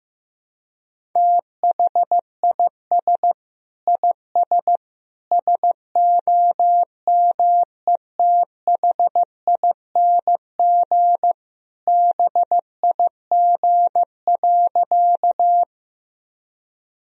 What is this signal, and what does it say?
Morse code, heard as a keyed tone, about 15 wpm, 700 Hz. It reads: THIS IS SOMETHING BIG.
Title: Sentences from Top 1000 Words - Review Part 1 - 15wpm